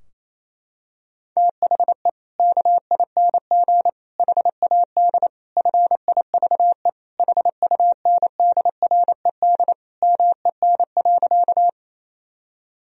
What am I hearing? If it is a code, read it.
Morse code, heard as a keyed tone, about 28 wpm, 700 Hz. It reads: THE KING HAD FIVE HUNDRED MEN.
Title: Sentences from Top 1000 Words - Review Part 2 - 28wpm